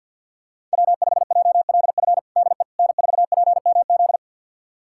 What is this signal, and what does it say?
Morse code, heard as a keyed tone, about 53 wpm, 700 Hz. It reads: W5JLF DE N4PMZ
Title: USA Callsign DE Callsign - 5 Characters - 50wpm